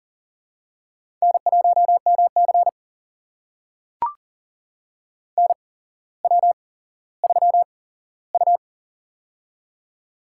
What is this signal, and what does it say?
Morse code, heard as a keyed tone, about 40 wpm, spaced as for 17 wpm, 700 Hz. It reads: N1MC NW3U
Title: USA Call Signs - 4 Characters - 8x Character Spacing - 40wpm